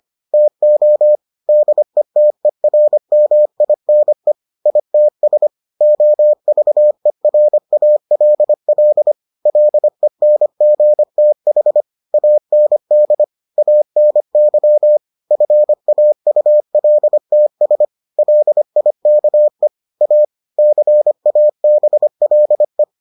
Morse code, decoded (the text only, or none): TO DETERMINE ITS OVERALL LENGTH AND ANY FAULTS LIKE A CABLE